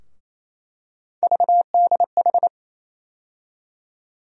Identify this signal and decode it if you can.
Morse code, heard as a keyed tone, about 28 wpm, 700 Hz. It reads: VDH
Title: USA Call Sign Postfixes - 3 Letters 28wpm